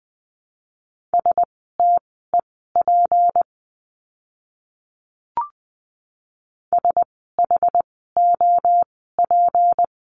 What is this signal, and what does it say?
Morse code, heard as a keyed tone, about 20 wpm, spaced as for 14 wpm, 700 Hz. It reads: STEP SHOP
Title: 153 Lesson 25f - Letter p - Words 20wpm